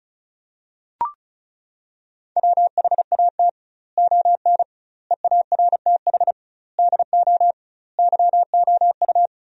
Morse code, heard as a keyed tone, about 36 wpm, 700 Hz. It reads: WHAT ON EARTH DO YOU
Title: 2x Word-Spacing - Sentences from Top 200 Words - 35wpm